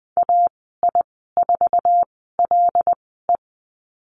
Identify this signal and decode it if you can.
Morse code, heard as a keyed tone, about 20 wpm, spaced as for 14 wpm, 700 Hz. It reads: AI4LE
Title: 088 Lesson 15g - Number 5 - Call Signs 20wpm